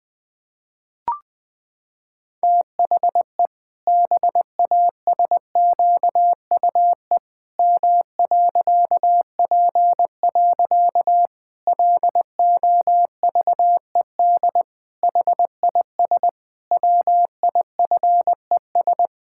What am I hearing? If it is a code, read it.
Morse code, heard as a keyed tone, about 21 wpm, 700 Hz. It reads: THE BASQUE M.P. LOVED HIS WIFES
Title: Pangram Challenge v2 - 20wpm